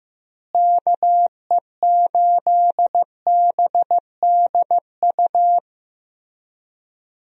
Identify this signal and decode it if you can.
Morse code, heard as a keyed tone, about 15 wpm, 700 Hz. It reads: KE8BDU
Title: USA Call Signs - 6 Characters - Repeated 3x - 15wpm